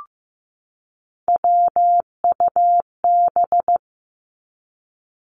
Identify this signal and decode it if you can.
Morse code, heard as a keyed tone, about 15 wpm, 700 Hz. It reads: WUB
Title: Three Challenging Letter-Number Combinations 15wpm